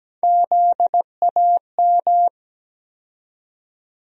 Morse code, heard as a keyed tone, about 17 wpm, 700 Hz. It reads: ZAM
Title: USA Call Sign Postfixes - 3 Letters 17wpm